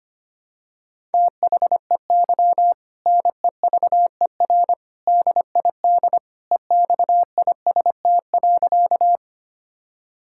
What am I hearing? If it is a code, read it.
Morse code, heard as a keyed tone, about 25 wpm, 700 Hz. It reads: THEY NEVER DID EXIST.
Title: Sentences from Top 1000 Words - Review Part 5 - 25wpm